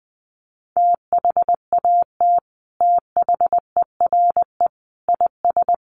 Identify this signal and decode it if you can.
Morse code, heard as a keyed tone, about 20 wpm, 700 Hz. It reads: THAT THERE IS
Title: Sets of 4 Words 20wpm